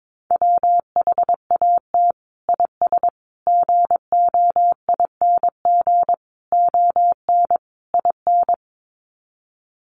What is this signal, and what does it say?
Morse code, heard as a keyed tone, about 22 wpm, 700 Hz. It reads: WHAT IS GOING ON IN
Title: Sets of 5 Words Encore2 22wpm